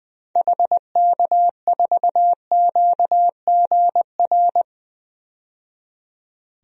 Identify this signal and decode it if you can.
Morse code, heard as a keyed tone, about 20 wpm, 700 Hz. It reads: HK4QGR
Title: DXCC-100 Call Signs - 6 Characters - 20wpm